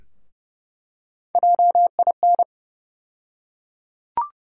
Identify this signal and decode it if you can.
Morse code, heard as a keyed tone, about 30 wpm, 700 Hz. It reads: JIN
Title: USA Call Sign Postfixes - 3 Letters 30wpm